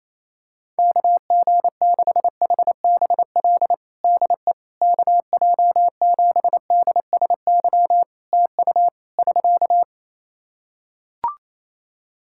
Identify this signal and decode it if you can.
Morse code, heard as a keyed tone, about 27 wpm, 700 Hz. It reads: KG6HBL DE KJ7DSY TU <SK>
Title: K1USN SST QSO Practice 28wpm